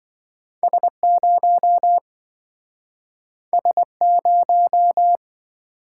Morse code, heard as a keyed tone, about 27 wpm, 700 Hz. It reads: S0 S0